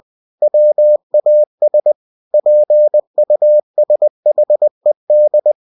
Morse code, heard as a keyed tone, about 20 wpm, 600 Hz. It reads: WAS PUSHED
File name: cw-books / winnie-the-pooh / winnie-20wpm-page-0018